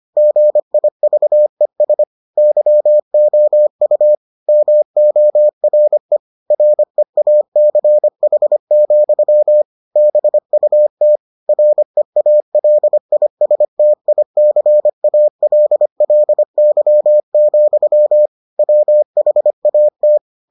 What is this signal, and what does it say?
Morse code, heard as a keyed tone, about 25 wpm, 600 Hz. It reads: GIVES YOU MORE REACH, BUT REALISTICALLY, WHAT